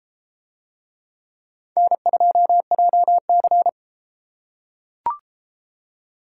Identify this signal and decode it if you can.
Morse code, heard as a keyed tone, about 32 wpm, 700 Hz. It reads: N2JC